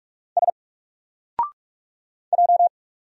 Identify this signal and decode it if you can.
Morse code, heard as a keyed tone, about 44 wpm, 700 Hz. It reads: S J
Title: Single Character (Letter:Number:Punctuation:Prosign) Alt 45wpm